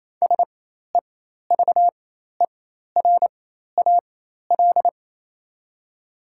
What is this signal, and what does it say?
Morse code, heard as a keyed tone, about 28 wpm, spaced as for 13 wpm, 700 Hz. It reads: SEVERAL
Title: 4x Character Spacing - Top 500 Words - 28wpm